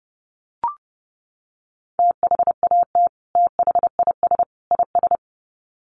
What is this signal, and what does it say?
Morse code, heard as a keyed tone, about 30 wpm, 700 Hz. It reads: THAT THIS IS
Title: Sets of 3 Words 30wpm